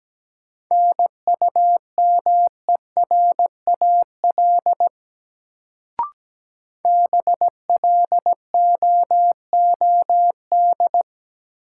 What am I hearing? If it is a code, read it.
Morse code, heard as a keyed tone, about 17 wpm, 700 Hz. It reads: NUMERAL BLOOD
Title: Sentences from Top 1000 Words - Review Part 2 - New Word Drill - 17wpm